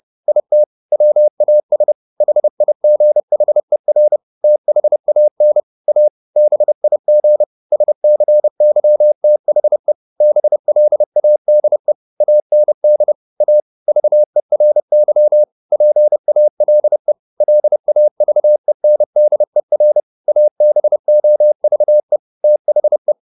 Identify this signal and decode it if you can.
Morse code, heard as a keyed tone, about 30 wpm, 600 Hz. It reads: IT WAS HIGHER THAN A BIG SCYTHE BLADE AND A VERY PALE LAVENDER ABOVE THE